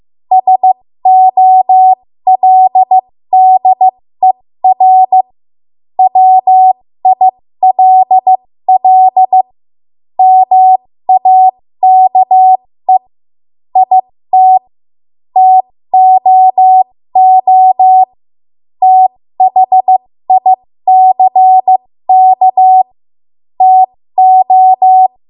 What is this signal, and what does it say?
Morse code, heard as a keyed tone, about 15 wpm, spaced as for 13 wpm, 750 Hz. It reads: SOLDER WILL MAKE IT TOO THICK TO
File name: Morse / 13WPM